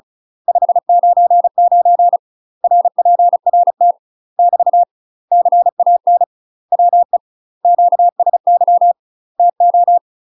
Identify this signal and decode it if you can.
Morse code, heard as a keyed tone, about 34 wpm, 700 Hz. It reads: 599 RPRT <BT> CAN WE QSY TO
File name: UK MASTER QSOs TRAINING / 35WPM / UK MASTER QSOs 0015